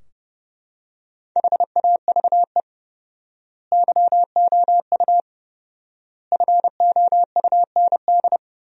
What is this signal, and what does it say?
Morse code, heard as a keyed tone, about 32 wpm, 700 Hz. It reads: HAVE YOU FOUND
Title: 4x Word-Spacing - Sentences from Top 300 Words - 30wpm